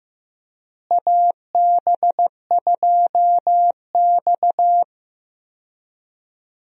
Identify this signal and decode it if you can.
Morse code, heard as a keyed tone, about 15 wpm, 700 Hz. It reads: AB2X
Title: USA Call Signs - 4 Characters - Repeated 2x - 15wpm